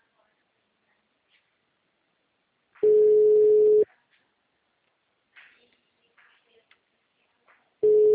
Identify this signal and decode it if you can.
Ringback tone (Europe, 425 Hz).
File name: out- -2004- - - .58